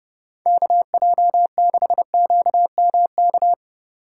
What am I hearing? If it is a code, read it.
Morse code, heard as a keyed tone, about 30 wpm, 700 Hz. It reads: KJ6QMK